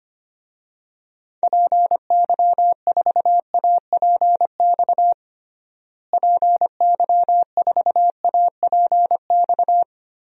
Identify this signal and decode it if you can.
Morse code, heard as a keyed tone, about 25 wpm, 700 Hz. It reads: PY4APX PY4APX
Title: DXCC-100 Call Signs - 6 Characters - Repeated 3x - 25wpm